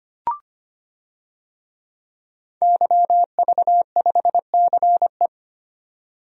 Morse code, heard as a keyed tone, about 25 wpm, 700 Hz. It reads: YV5CE
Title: DXCC-100 Call Signs - 5 Characters - 25wpm